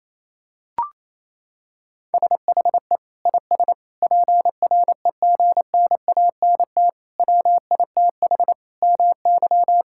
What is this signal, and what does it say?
Morse code, heard as a keyed tone, about 28 wpm, 700 Hz. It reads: SHE IS PREGNANT WITH MY
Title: Sentences from Top 1000 Words - Review Part 6 - 28wpm